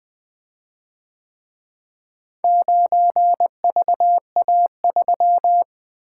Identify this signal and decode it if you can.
Morse code, heard as a keyed tone, about 24 wpm, 700 Hz. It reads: 9VA3
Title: Speed-Racing - Four Letter-Number Combinations - 20wpm